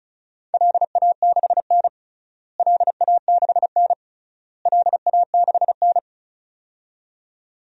Morse code, heard as a keyed tone, about 36 wpm, 700 Hz. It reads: LA6N LA6N LA6N
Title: DXCC-100 Call Signs - 4 Characters - Repeated 3x - 35wpm